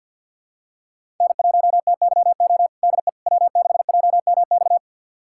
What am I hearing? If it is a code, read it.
Morse code, heard as a keyed tone, about 50 wpm, 700 Hz. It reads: N1TYO DE W6JMX
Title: USA Callsign DE Callsign - 5 Characters - 50wpm